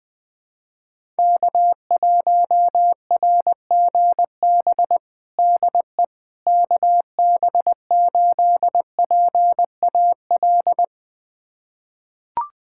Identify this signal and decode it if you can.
Morse code, heard as a keyed tone, about 20 wpm, 700 Hz. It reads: K1RGB DE KB8PAL
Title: USA Callsign DE Callsign 20wpm